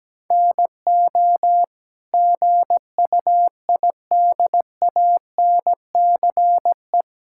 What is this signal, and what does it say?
Morse code, heard as a keyed tone, about 17 wpm, 700 Hz. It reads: NO GUIDANCE
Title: Sentences from Top 1600 Words 17wpm